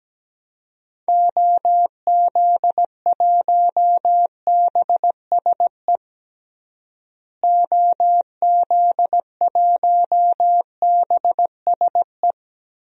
Morse code, heard as a keyed tone, about 17 wpm, 700 Hz. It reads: OZ1BSE OZ1BSE
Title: DXCC-100 Call Signs - 6 Characters - Repeated 3x - 17wpm